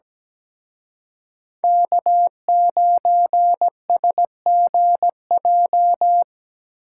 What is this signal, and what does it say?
Morse code, heard as a keyed tone, about 19 wpm, 700 Hz. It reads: K9SGJ
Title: Speed Racing - USA 1x3 Call Signs 17wpm